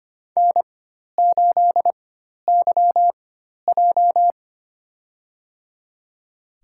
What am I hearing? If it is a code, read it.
Morse code, heard as a keyed tone, about 25 wpm, spaced as for 12 wpm, 700 Hz. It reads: N8YJ
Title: USA Call Signs - 4x Character Spacing - 25wpm